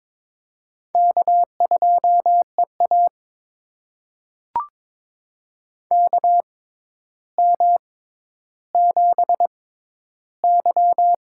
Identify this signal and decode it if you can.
Morse code, heard as a keyed tone, about 22 wpm, spaced as for 11 wpm, 700 Hz. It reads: K2EA KM7Y
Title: USA Call Signs - 4 Characters - 6x Character Spacing - 22wpm